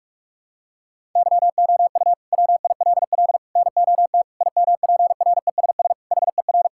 Morse code, heard as a keyed tone, about 46 wpm, 700 Hz. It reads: YOU WILL NOT IMPRESS HER
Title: Sentences from Top 1600 Words 45wpm